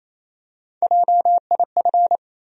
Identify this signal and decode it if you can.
Morse code, heard as a keyed tone, about 28 wpm, 700 Hz. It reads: JIF
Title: USA Call Sign Postfixes - 3 Letters 28wpm